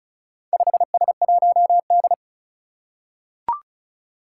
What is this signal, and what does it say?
Morse code, heard as a keyed tone, about 36 wpm, 700 Hz. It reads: 5S1D
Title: Four Challenging Letter-Number Combinations 35wpm